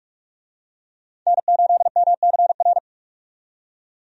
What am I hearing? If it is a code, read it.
Morse code, heard as a keyed tone, about 42 wpm, 700 Hz. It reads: N8MCR
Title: USA Call Signs - 5 Characters - 45wpm